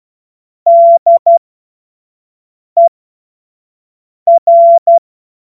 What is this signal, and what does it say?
Morse code, heard as a keyed tone, about 12 wpm, spaced as for 5 wpm, 680 Hz. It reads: DER
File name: Morsekurs / Witze / Witz 07. Juni 2022 25 F70